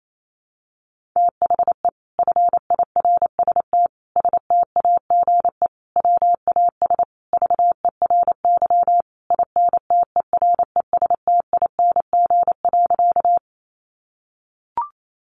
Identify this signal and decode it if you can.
Morse code, heard as a keyed tone, about 28 wpm, 700 Hz. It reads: THE FIRST STAGE WAS VERY INTERESTING.
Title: Sentences from Top 1200 Words 28wpm